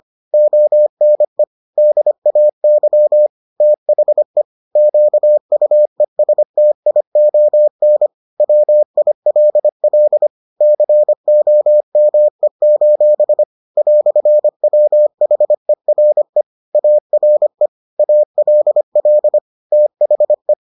Morse code, heard as a keyed tone, about 25 wpm, 600 Hz. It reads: ONE DAY THE QUESTION WILL COME: "WHERE ARE ALL THE